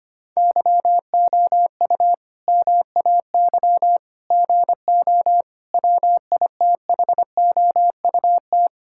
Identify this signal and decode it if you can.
Morse code, heard as a keyed tone, about 25 wpm, 700 Hz. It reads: YOU MAY GO WITHOUT
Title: Sentences from Top 1600 Words 25wpm